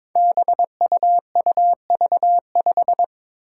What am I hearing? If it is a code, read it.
Morse code, heard as a keyed tone, about 22 wpm, 700 Hz. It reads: BUUV5
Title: Five Challenging Letter-Number Combinations 22wpm